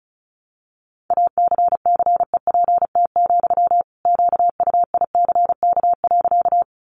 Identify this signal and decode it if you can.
Morse code, heard as a keyed tone, about 34 wpm, 700 Hz. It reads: ACCEPT, QUICK.
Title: Sentences from Top 1300 Words 35wpm